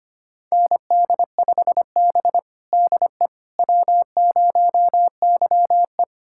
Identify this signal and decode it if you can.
Morse code, heard as a keyed tone, about 25 wpm, 700 Hz. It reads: ND5B DE W0YE
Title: USA Callsign DE Callsign - 4 Characters - 25wpm